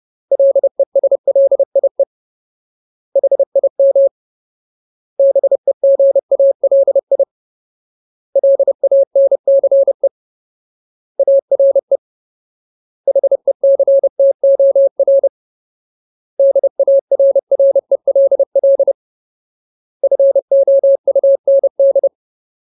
Morse code, heard as a keyed tone, about 30 wpm, 550 Hz. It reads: LESLIE HIM BEGALI LANCE ARE HECTOR DARRELL FOUND
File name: Ultimate Word List / 30WPM Ultimate